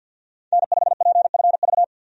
Morse code, heard as a keyed tone, about 52 wpm, 700 Hz. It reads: N5PFV
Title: USA Call Signs - 5 Characters - 50wpm